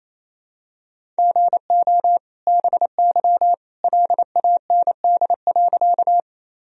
Morse code, heard as a keyed tone, about 28 wpm, 700 Hz. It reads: GO BY LAND.
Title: Sentences from Top 1000 Words - Review Part 1 - 28wpm